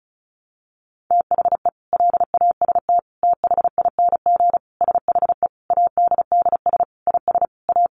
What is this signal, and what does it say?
Morse code, heard as a keyed tone, about 34 wpm, 700 Hz. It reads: THE LAST THING SHE ADDS IS A
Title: Easy Phrases 35wpm